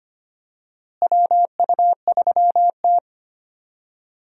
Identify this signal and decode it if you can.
Morse code, heard as a keyed tone, about 25 wpm, 700 Hz. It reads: WU3T
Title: USA Call Signs - 4 Characters - Repeated 2x - 25wpm